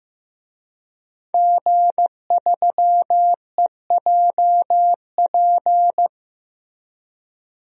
Morse code, heard as a keyed tone, about 15 wpm, 700 Hz. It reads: G3EJP
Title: DXCC-100 Call Signs - 5 Characters - Repeated 3x - 15wpm